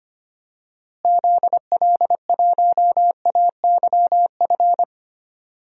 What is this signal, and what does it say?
Morse code, heard as a keyed tone, about 25 wpm, 700 Hz. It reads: ZL1AYF
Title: DXCC-100 Call Signs - 6 Characters - Repeated 3x - 25wpm